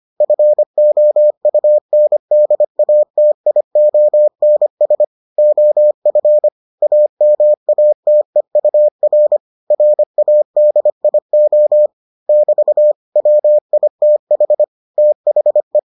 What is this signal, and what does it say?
Morse code, heard as a keyed tone, about 25 wpm, 600 Hz. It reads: FOUNDATIONS OF AMATEUR RADIO <BT> WITH THE